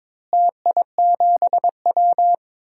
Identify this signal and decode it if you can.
Morse code, heard as a keyed tone, about 22 wpm, 700 Hz. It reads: TI7W
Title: DXCC-100 Call Signs - 4 Characters - Repeated 3x - 22wpm